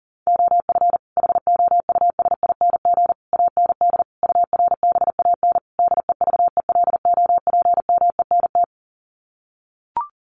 Morse code, heard as a keyed tone, about 37 wpm, 700 Hz. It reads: OF HOUSING AND URBAN DEVELOPMENT
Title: Sets of 5 Words Encore2 40wpm